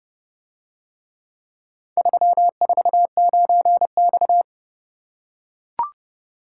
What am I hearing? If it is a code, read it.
Morse code, heard as a keyed tone, about 29 wpm, 700 Hz. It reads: 349X